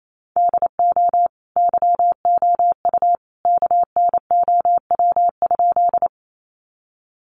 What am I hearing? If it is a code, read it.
Morse code, heard as a keyed tone, about 28 wpm, 700 Hz. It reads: DO YOU KNOW?
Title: Sentences from Top 100 Words 28wpm